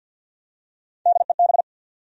Morse code, heard as a keyed tone, about 52 wpm, 700 Hz. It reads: DEB
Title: USA Common Names - Max 3 Letters - 50wpm